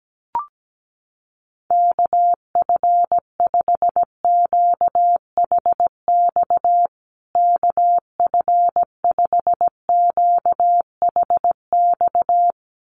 Morse code, heard as a keyed tone, about 17 wpm, 700 Hz. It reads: KF5QHX KF5QHX
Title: USA Call Signs Repeated 2x 17wpm